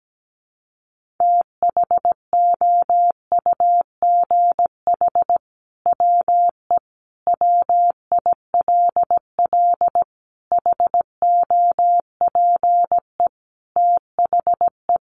Morse code, heard as a keyed tone, about 17 wpm, 700 Hz. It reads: THOUGH WE WILL HOPE THE
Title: Sentences from Top 1000 Words 17wpm